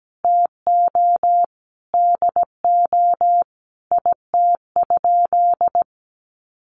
Morse code, heard as a keyed tone, about 17 wpm, 700 Hz. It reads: TO DO IT?
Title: Sentences from Top 1200 Words 17wpm